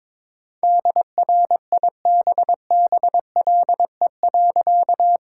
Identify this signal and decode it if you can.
Morse code, heard as a keyed tone, about 22 wpm, 700 Hz. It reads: DRIBBLE.
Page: morse-code-ninja-practice-22wpm